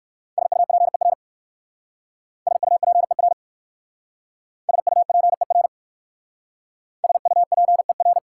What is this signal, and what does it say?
Morse code, heard as a keyed tone, about 59 wpm, 690 Hz. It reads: SUPER SUPER SUPER SUPER